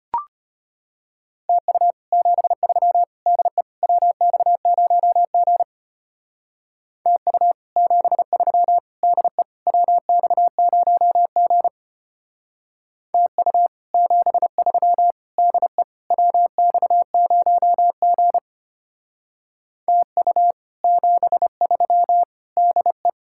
Morse code, heard as a keyed tone, about 31 wpm, 700 Hz. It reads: TU 73 DE WX0G TU 73 DE WX0G TU 73 DE WX0G TU 73 DE
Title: Speed-Racing Common QSO Phrases - 25wpm